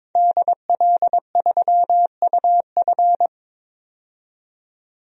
Morse code, heard as a keyed tone, about 22 wpm, 700 Hz. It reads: DL3UF
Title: DXCC-100 Call Signs - 5 Characters - 22wpm